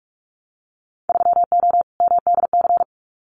Morse code, heard as a keyed tone, about 46 wpm, 700 Hz. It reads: 3O MDC
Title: Winter Field Day Exchanges 45wpm